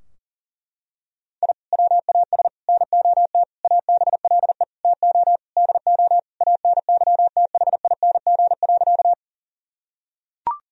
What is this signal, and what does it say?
Morse code, heard as a keyed tone, about 37 wpm, 700 Hz. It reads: I WAS NOT ABLE TO DO ANYTHING.
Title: Sentences from Top 1000 Words - Review Part 2 - 40wpm